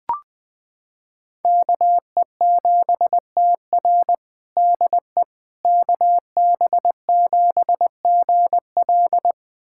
Morse code, heard as a keyed tone, about 20 wpm, 700 Hz. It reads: KE7TR DE KB7GL
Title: USA Callsign DE Callsign - 5 Characters - 20wpm